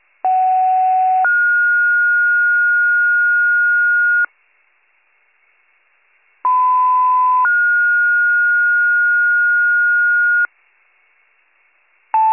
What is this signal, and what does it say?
Two-tone page, heard as a one-second tone, then a three-second tone, about 750 then 1430 Hz, about 1010 then 1430 Hz. It is heard from 0.2 to 10.5 s.